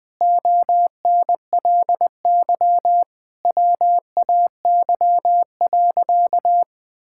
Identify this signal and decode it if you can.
Morse code, heard as a keyed tone, about 19 wpm, 700 Hz. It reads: ONLY WAY.
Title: Sentences with 2-Word Phrases - 20wpm